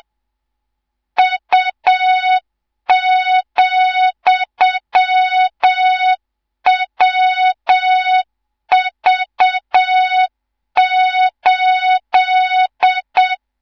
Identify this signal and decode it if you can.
Morse code, heard as a keyed tone, about 7 wpm, 3700 Hz, with noise. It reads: U,WV8
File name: Morze-HA2MN / hang / 18 IRASJELEK2